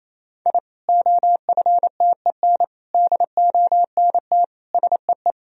Morse code, heard as a keyed tone, about 28 wpm, 700 Hz. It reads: I OFTEN DONT SEE
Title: Sentences from Top 1000 Words - Review Part 1 - 28wpm